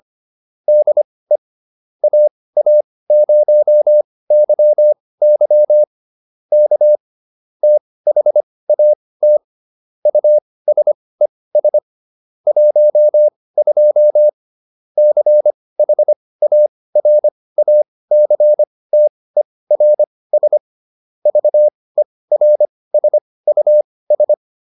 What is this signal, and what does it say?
Morse code, heard as a keyed tone, about 25 wpm, spaced as for 18 wpm, 600 Hz. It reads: DE AA0YY K THAT USES 12 CHARACTERS VERSUS